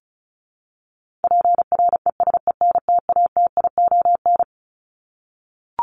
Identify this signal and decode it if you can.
Morse code, heard as a keyed tone, about 34 wpm, 700 Hz. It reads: PRESENTATION
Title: Top 1501-1750 Words 35wpm